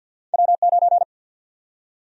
Morse code, heard as a keyed tone, about 53 wpm, 700 Hz. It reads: W9
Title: USA Call Sign Prefixes - 50wpm